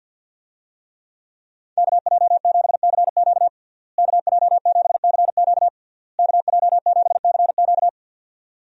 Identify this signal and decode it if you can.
Morse code, heard as a keyed tone, about 52 wpm, 700 Hz. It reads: KJ7CQ KJ7CQ KJ7CQ
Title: USA Call Signs - 5 Characters - Repeated 3x - 50wpm